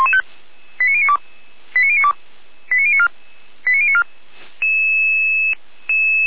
Five-tone selective call (ZVEI), heard as quick runs of five tones, tones of about 70 ms. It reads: 89092 89092 89094 89094
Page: SDR/ZVEI